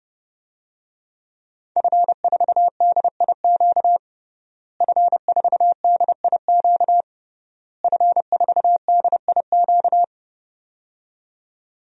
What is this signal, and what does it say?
Morse code, heard as a keyed tone, about 32 wpm, 700 Hz. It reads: F4DIQ F4DIQ F4DIQ
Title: DXCC-100 Call Signs - 5 Characters - Repeated 3x - 30wpm